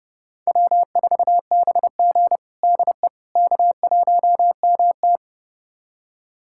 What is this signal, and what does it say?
Morse code, heard as a keyed tone, about 31 wpm, 700 Hz. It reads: W4BG DE K1MT
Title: USA Callsign DE Callsign - 4 Characters - 30wpm